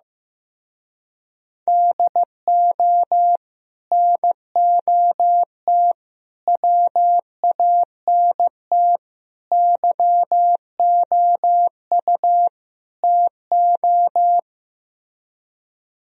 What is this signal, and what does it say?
Morse code, heard as a keyed tone, about 15 wpm, 700 Hz. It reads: DO NOT WANT YOU TO
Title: Speed-Racing - Sets of 5 Words 15wpm